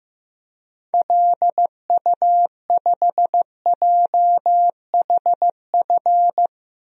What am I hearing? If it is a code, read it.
Morse code, heard as a keyed tone, about 15 wpm, 700 Hz. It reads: LU5JHF
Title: DXCC-100 Call Signs - 6 Characters - 15wpm